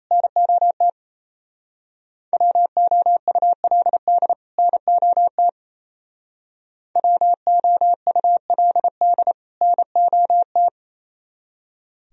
Morse code, heard as a keyed tone, about 31 wpm, 700 Hz. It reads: NOT WOULD NOT WOULD NOT
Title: Speed-Racing - Sets of 2 Words 28wpm